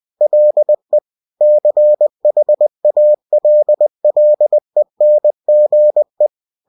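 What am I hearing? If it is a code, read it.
Morse code, heard as a keyed tone, about 20 wpm, 600 Hz. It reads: LE CHALLENGE
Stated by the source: Bulletin sent in CW.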